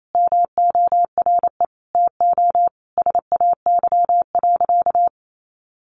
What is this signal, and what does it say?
Morse code, heard as a keyed tone, about 28 wpm, 700 Hz. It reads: MORE TO SAY.
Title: Sentences from Top 600 Words 28wpm